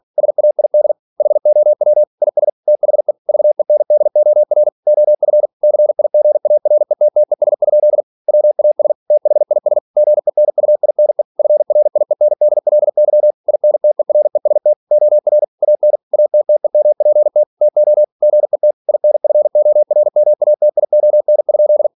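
Morse code, heard as a keyed tone, about 46 wpm, 600 Hz. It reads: SAID HOW IS THE VENDOR OF CIGARETTES? WAS THIS GENUINE FRIENDLY INTEREST OR AN ATTEMPT TO GET INFORMATION?
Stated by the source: Practice text from an English novel sent at 47 WPM.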